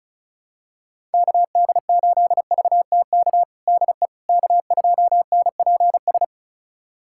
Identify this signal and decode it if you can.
Morse code, heard as a keyed tone, about 36 wpm, 700 Hz. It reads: KD8VTK DE K2NPS
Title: USA Callsign DE Callsign 35wpm